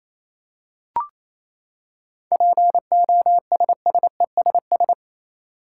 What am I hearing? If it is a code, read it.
Morse code, heard as a keyed tone, about 28 wpm, 700 Hz. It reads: POSSESS
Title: Seven Letter Words - 28wpm